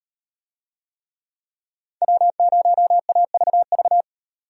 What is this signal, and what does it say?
Morse code, heard as a keyed tone, about 37 wpm, 700 Hz. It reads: W0AVV